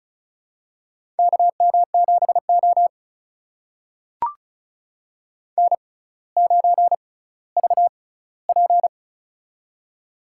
Morse code, heard as a keyed tone, about 34 wpm, spaced as for 18 wpm, 700 Hz. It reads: KM7O N9VP
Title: USA Call Signs - 6x Character Spacing - 35wpm